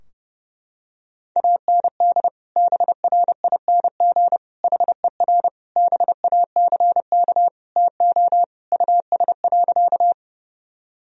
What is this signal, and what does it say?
Morse code, heard as a keyed tone, about 30 wpm, 700 Hz. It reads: AND BRING HER BACK TO US.
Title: Sentences from Top 1000 Words - Review Part 3 - 30wpm